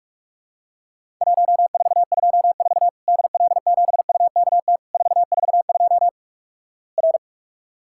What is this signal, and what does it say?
Morse code, heard as a keyed tone, about 46 wpm, 700 Hz. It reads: 1424 DL7UKT 442 R
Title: Worked All Europe DX Contest - QTC Exchange - 45wpm